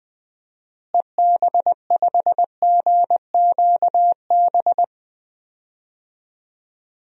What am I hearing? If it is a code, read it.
Morse code, heard as a keyed tone, about 20 wpm, 700 Hz. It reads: EB5GQB
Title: DXCC-100 Call Signs - 6 Characters - Repeated 3x - 20wpm